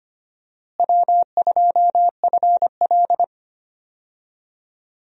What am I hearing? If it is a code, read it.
Morse code, heard as a keyed tone, about 25 wpm, 700 Hz. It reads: W2FL
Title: USA Call Signs - 4 Characters - Repeated 2x - 25wpm